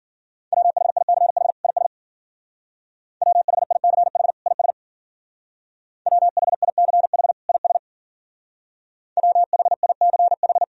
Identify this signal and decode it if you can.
Morse code, heard as a keyed tone, about 49 wpm, 690 Hz. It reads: WHICH IS WHICH IS WHICH IS WHICH
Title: Speed-Racing - Sets of 2 Words 40wpm